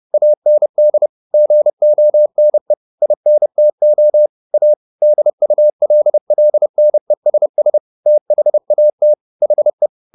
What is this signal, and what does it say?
Morse code, heard as a keyed tone, about 30 wpm, 600 Hz. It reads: AND GONE INTO A DULLNESS THAT HE